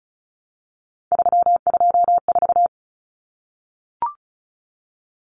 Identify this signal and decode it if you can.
Morse code, heard as a keyed tone, about 34 wpm, 700 Hz. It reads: 324
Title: Numbers - 3-Digit 35wpm